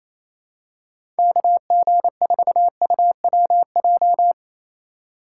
Morse code, heard as a keyed tone, about 28 wpm, 700 Hz. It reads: KG4UWJ
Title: USA Call Signs - 6 Characters - 8x Character Spacing - 28wpm